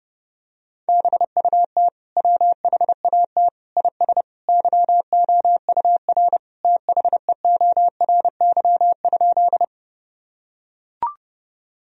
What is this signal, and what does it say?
Morse code, heard as a keyed tone, about 30 wpm, 700 Hz. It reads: BUT WHAT IS YOUR THEORY?
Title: Sentences from Top 1000 Words - Review Part 3 - 30wpm